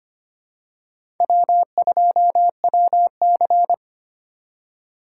Morse code, heard as a keyed tone, about 25 wpm, 700 Hz. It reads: W2WC
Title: USA Call Signs - 4 Characters - 3x Character Spacing - 25wpm